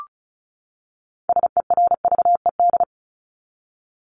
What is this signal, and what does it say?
Morse code, heard as a keyed tone, about 34 wpm, 700 Hz. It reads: SERVED
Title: ED Suffix - Top 100 Words with Base Words 35wpm